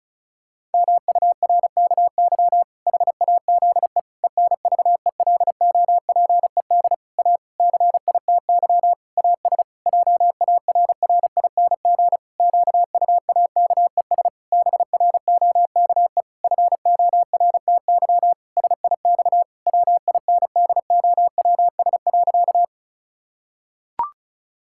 Morse code, heard as a keyed tone, about 34 wpm, 700 Hz. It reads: MURKY HAZE ENVELOPED A CITY AS JARRING QUAKES BROKE FORTY SIX WINDOWS.
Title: Pangram Challenge v2 - 35wpm